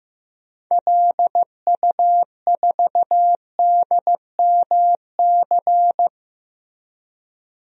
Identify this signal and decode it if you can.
Morse code, heard as a keyed tone, about 15 wpm, 700 Hz. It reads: LU4DMC
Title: DXCC-100 Call Signs - 6 Characters - 15wpm